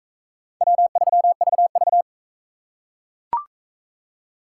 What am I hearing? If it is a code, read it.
Morse code, heard as a keyed tone, about 37 wpm, 700 Hz. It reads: W3VV